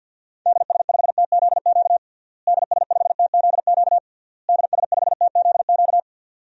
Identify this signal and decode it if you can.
Morse code, heard as a keyed tone, about 51 wpm, 700 Hz. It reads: DS5TZQ DS5TZQ DS5TZQ
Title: DXCC-100 Call Signs - 6 Characters - Repeated 3x - 50wpm